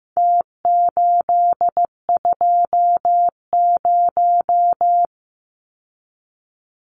Morse code, heard as a keyed tone, about 15 wpm, 700 Hz. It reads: T820
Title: Numbers - Contest Style - 4-Digit 15wpm